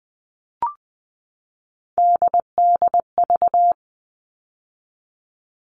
Morse code, heard as a keyed tone, about 20 wpm, 700 Hz. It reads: DDV
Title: All Cut Numbers - 3-Digit 20wpm